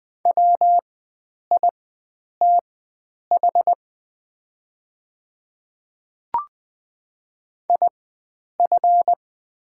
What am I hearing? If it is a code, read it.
Morse code, heard as a keyed tone, about 20 wpm, spaced as for 9 wpm, 700 Hz. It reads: WITH IF
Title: 4x Character Spacing - Top 300 Words - 20wpm